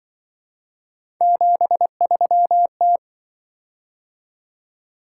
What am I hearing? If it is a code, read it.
Morse code, heard as a keyed tone, about 25 wpm, 700 Hz. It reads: 73T